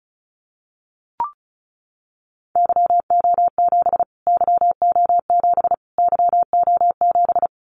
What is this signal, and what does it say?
Morse code, heard as a keyed tone, about 34 wpm, 700 Hz. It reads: YO7 YO7 YO7
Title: Callsign Prefixes for Top 25 DXCC - Repeated 3x - 35wpm